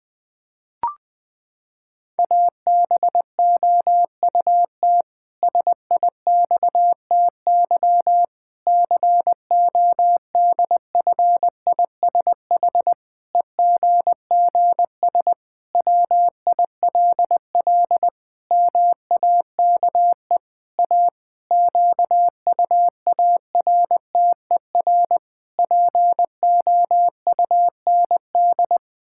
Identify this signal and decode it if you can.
Morse code, heard as a keyed tone, about 20 wpm, 700 Hz. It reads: ABOUT SIXTY CODFISH EGGS WILL MAKE A QUARTER POUND
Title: Pangram Challenge v2 - 20wpm